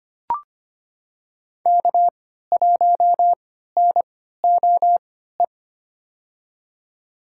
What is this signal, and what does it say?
Morse code, heard as a keyed tone, about 25 wpm, spaced as for 14 wpm, 700 Hz. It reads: K1NOE
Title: USA Call Signs - 5 Characters - 3x Character Spacing - 25wpm